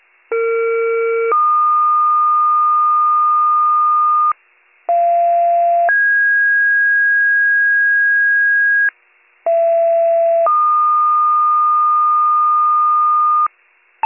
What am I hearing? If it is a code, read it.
Two-tone page, heard as a one-second tone, then a three-second tone, about 460 then 1220 Hz, about 690 then 1650 Hz, about 670 then 1220 Hz. It is heard from 0.3 to 13.5 s.